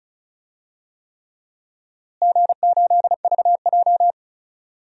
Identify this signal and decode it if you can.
Morse code, heard as a keyed tone, about 36 wpm, 700 Hz. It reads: G8VJ
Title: DXCC-100 Call Signs - 4 Characters - Repeated 3x - 35wpm